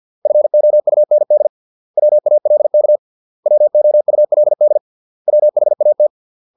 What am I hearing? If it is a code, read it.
Morse code, heard as a keyed tone, about 50 wpm, 600 Hz. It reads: FOUND WALK WOULD WHAT